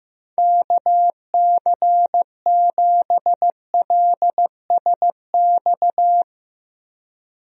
Morse code, heard as a keyed tone, about 14 wpm, 700 Hz. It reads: KC7LSX
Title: USA Call Signs - 6 Characters - Repeated 3x - 15wpm